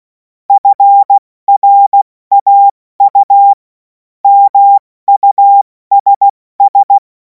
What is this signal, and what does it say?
Morse code, heard as a keyed tone, about 16 wpm, 800 Hz. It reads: FRAU MUSS
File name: Morsekurs / Witze / Witz 26. Mai 2026 70 F80